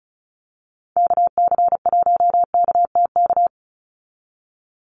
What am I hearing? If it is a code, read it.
Morse code, heard as a keyed tone, about 36 wpm, 700 Hz. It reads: KC1KTK
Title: USA Call Signs 35wpm